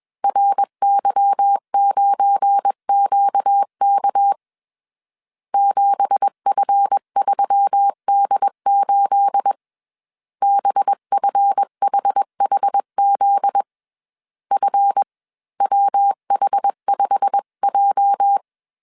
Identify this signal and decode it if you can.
Morse code, heard as a keyed tone, about 21 wpm, 800 Hz. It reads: RY9QK 7F3D8 BFHHZ FWH5J
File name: CW / DET / 1 9 1500C S C 20wpm 2310